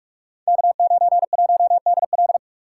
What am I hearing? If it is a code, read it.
Morse code, heard as a keyed tone, about 44 wpm, 690 Hz. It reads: K91DL
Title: Code Groups - 5 Letter:Numbers - 45wpm